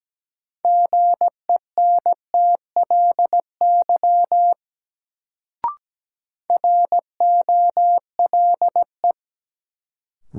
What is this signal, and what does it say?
Morse code, heard as a keyed tone, about 17 wpm, 700 Hz. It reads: GENTLY ROLE
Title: Sentences from Top 1500 Words - New Word Drill Alt - 17wpm